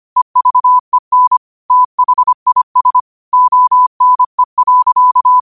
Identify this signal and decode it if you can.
Morse code, heard as a keyed tone, about 25 wpm, 1000 Hz. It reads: EVEN THIS ONE.
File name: qotdfiles / 25wpm